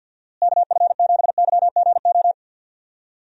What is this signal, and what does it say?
Morse code, heard as a keyed tone, about 50 wpm, 700 Hz. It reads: KF7YGO
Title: USA Call Signs - 6 Characters - Repeated 2x - 50wpm